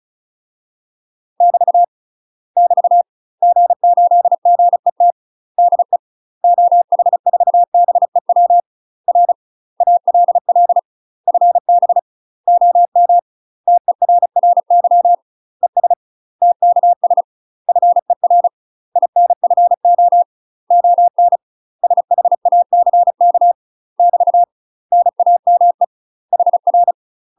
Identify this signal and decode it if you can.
Morse code, heard as a keyed tone, about 33 wpm, 700 Hz. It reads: <BT> <BT> G8GET DE OH4BEW R ALL FB OM TERRY ES TKS FER INFO ON SHACK <BT> NAME HR
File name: UK MASTER QSOs TRAINING / 35WPM / UK MASTER QSOs 0014